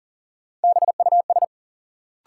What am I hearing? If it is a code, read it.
Morse code, heard as a keyed tone, about 40 wpm, 700 Hz. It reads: BUS
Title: Sentences from Top 1000 Words - Review Part 6 - New Word Drill - 40wpm